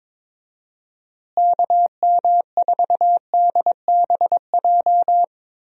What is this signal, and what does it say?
Morse code, heard as a keyed tone, about 22 wpm, 700 Hz. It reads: KM4DBJ
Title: USA Call Signs - 6 Characters - Repeated 2x - 22wpm